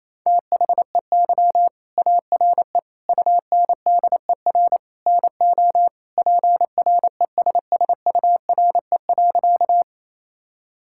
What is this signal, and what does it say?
Morse code, heard as a keyed tone, about 27 wpm, 700 Hz. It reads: THEY ARE UNDER NO PRESSURE.
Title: Sentences from Top 1000 Words - Review Part 6 - 28wpm